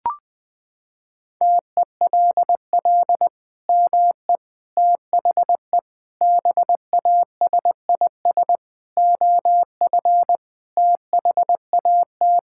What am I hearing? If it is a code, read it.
Morse code, heard as a keyed tone, about 20 wpm, 700 Hz. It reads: TELL ME THE BASIS OF THAT
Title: Sentences from Top 1000 Words - Review Part 5 - 20wpm